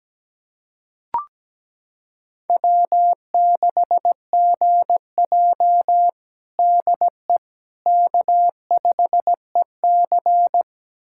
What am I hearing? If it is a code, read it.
Morse code, heard as a keyed tone, about 17 wpm, 700 Hz. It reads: W6GJ DE K5EC
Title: USA Callsign DE Callsign - 4 Characters - 17wpm